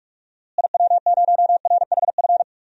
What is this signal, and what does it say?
Morse code, heard as a keyed tone, about 46 wpm, 700 Hz. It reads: IW0RHF
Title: DXCC-100 Call Signs - 6 Characters - Repeated 3x - 45wpm